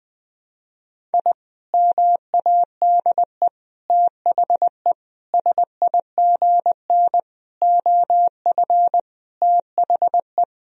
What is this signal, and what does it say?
Morse code, heard as a keyed tone, about 20 wpm, 700 Hz. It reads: I MADE THE SIGN OF THE
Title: Sentences from Top 1000 Words - Review Part 1 - 20wpm